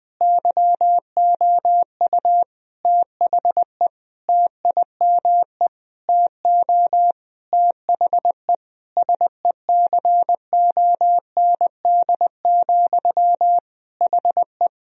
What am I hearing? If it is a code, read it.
Morse code, heard as a keyed tone, about 20 wpm, 700 Hz. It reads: YOU THE TIME TO THE SECOND, HE
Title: Sentences from Top 1000 Words - Review Part 1 - 20wpm